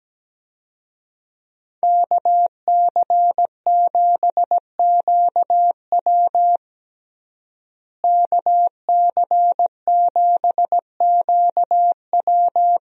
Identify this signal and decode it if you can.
Morse code, heard as a keyed tone, about 17 wpm, 700 Hz. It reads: KC7QW KC7QW
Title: USA Call Signs - 5 Characters - Repeated 3x - 17wpm